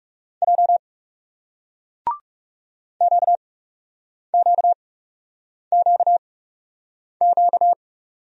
Morse code, heard as a keyed tone, about 42 wpm, spaced as for 8 wpm, 700 Hz. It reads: J QQQQ